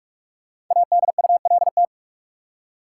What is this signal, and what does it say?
Morse code, heard as a keyed tone, about 45 wpm, 700 Hz. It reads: ADULT
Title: Five Letter Words - 45wpm